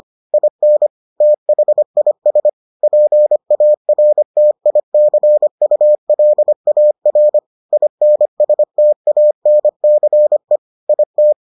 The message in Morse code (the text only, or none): IN THIS PARTICULAR INSTANCE IT